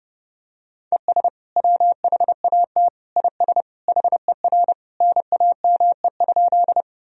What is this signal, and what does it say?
Morse code, heard as a keyed tone, about 30 wpm, 700 Hz. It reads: ES WHAT IS HER NAME?
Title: Sentences from Top 200 Words and Abbreviations 30wpm